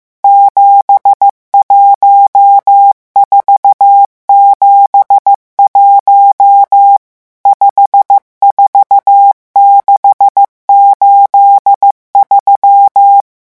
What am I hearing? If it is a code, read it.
Morse code, heard as a keyed tone, about 15 wpm, 800 Hz. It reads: 71471 54683
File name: morse mp3 / revision11 chiffres vitesse 14